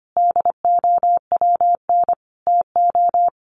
Morse code, heard as a keyed tone, about 25 wpm, 700 Hz. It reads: DOWN TO
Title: A Sound Of Thunder 25wpm